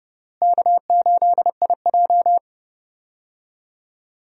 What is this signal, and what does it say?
Morse code, heard as a keyed tone, about 32 wpm, 700 Hz. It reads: K8IJ
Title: USA Call Signs - 4 Characters - Repeated 3x - 30wpm